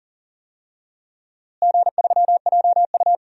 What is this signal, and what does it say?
Morse code, heard as a keyed tone, about 40 wpm, 700 Hz. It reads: G3JU
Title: DXCC-100 Call Signs - 4 Characters - Repeated 2x - 40wpm